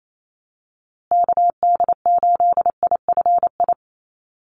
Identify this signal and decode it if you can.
Morse code, heard as a keyed tone, about 28 wpm, 700 Hz. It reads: KD8IFI
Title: USA Call Signs Repeated 2x 28wpm